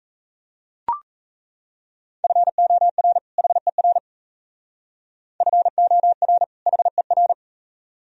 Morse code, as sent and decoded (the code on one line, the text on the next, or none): ..-. --- .-. / .... .-. / ..-. --- .-. / .... .-.
FOR HER FOR HER